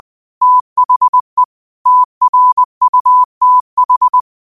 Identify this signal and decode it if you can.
Morse code, heard as a keyed tone, about 20 wpm, 1000 Hz. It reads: THE TRUTH